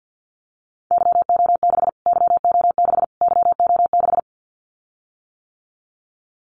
Morse code, heard as a keyed tone, about 49 wpm, 700 Hz. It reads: YO6 YO6 YO6
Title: Callsign Prefixes for Top 25 DXCC - Repeated 3x - 50wpm